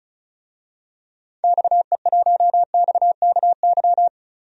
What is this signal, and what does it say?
Morse code, heard as a keyed tone, about 36 wpm, 700 Hz. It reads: XE1XKY
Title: DXCC-100 Call Signs - 6 Characters - Repeated 3x - 35wpm